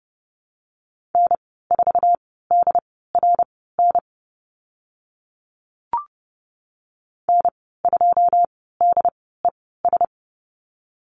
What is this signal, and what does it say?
Morse code, heard as a keyed tone, about 30 wpm, spaced as for 17 wpm, 700 Hz. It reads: N4DRN N2DES
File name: Lesson 15g - Number 5 - Call Signs 30wpm